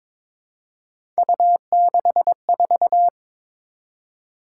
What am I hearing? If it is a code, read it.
Morse code, heard as a keyed tone, about 22 wpm, 700 Hz. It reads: U64
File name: Three Letter-Number Combinations - 22wpm